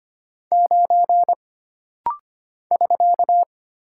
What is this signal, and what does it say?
Morse code, heard as a keyed tone, about 25 wpm, 700 Hz. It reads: 9 <SK>
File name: Single Character (Letter:Number:Punctuation:Prosign) - Rapid-Fire - 25wpm